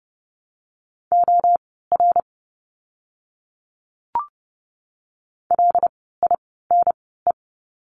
Morse code, heard as a keyed tone, about 30 wpm, spaced as for 17 wpm, 700 Hz. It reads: OR LINE
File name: course / 30:10wpm / 115 Lesson 19f - Letter w - Words 30wpm